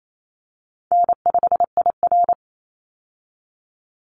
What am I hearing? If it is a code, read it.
Morse code, heard as a keyed tone, about 28 wpm, 700 Hz. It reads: N5IR
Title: Call Signs Repeated 3x - 28wpm